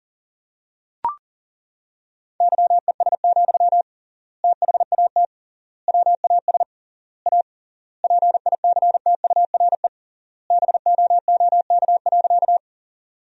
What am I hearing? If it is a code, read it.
Morse code, heard as a keyed tone, about 37 wpm, 700 Hz. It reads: YES, THAT WAS A PICTURE BOOK.
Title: 3x Word-Spacing - Sentences from Top 500 Words - 40wpm